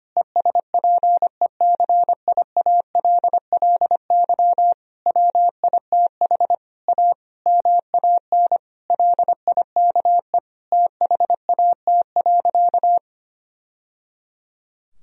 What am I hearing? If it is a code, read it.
Morse code, heard as a keyed tone, about 25 wpm, 700 Hz. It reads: ESPECIALLY WITH A MAN LIKE THAT.
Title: Sentences from Top 1000 Words - Review Part 4 - 25wpm